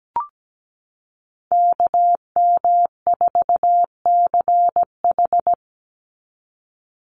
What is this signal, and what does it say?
Morse code, heard as a keyed tone, about 17 wpm, 700 Hz. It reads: KM4CH
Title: Call Signs Repeated 2x - 17wpm